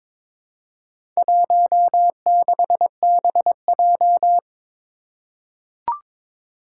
Morse code, heard as a keyed tone, about 22 wpm, 700 Hz. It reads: 16BJ
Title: Four Challenging Letter-Number Combinations 22wpm